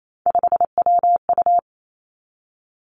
Morse code, heard as a keyed tone, about 28 wpm, 700 Hz. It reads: 5WU
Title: Three Challenging Letter-Number Combinations 28wpm